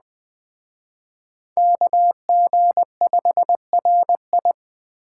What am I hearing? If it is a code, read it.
Morse code, heard as a keyed tone, about 19 wpm, 700 Hz. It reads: KG5RI
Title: Speed Racing - USA 2x2 Call Signs 15wpm